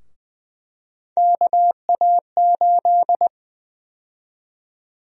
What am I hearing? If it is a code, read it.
Morse code, heard as a keyed tone, about 21 wpm, 700 Hz. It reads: KA8
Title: USA Call Sign Prefixes - Repeated 2x - 20wpm